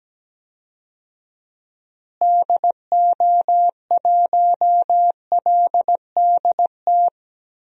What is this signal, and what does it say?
Morse code, heard as a keyed tone, about 17 wpm, 700 Hz. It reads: DO1LDT
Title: DXCC-100 Call Signs - 6 Characters - Repeated 2x - 17wpm